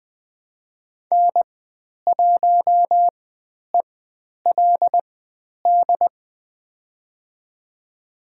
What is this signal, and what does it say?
Morse code, heard as a keyed tone, about 20 wpm, spaced as for 10 wpm, 700 Hz. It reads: N1ELD
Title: 095 Lesson 16g - Letter c - Call Signs 20wpm